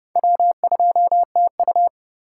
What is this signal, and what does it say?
Morse code, heard as a keyed tone, about 29 wpm, 700 Hz. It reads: W2TU
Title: USA Call Signs - 4x Character Spacing - 30wpm